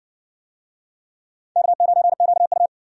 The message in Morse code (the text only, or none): K9QU